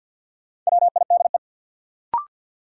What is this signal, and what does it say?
Morse code, heard as a keyed tone, about 53 wpm, 700 Hz. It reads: WIDE
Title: Sentences from Top 900 Words - New Word Drill - 50wpm